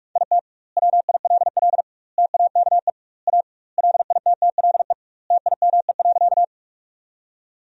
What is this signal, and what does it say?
Morse code, heard as a keyed tone, about 44 wpm, 700 Hz. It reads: IT WILL TAKE A LITTLE TIME.
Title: 2x Word-Spacing - Sentences from Top 200 Words - 45wpm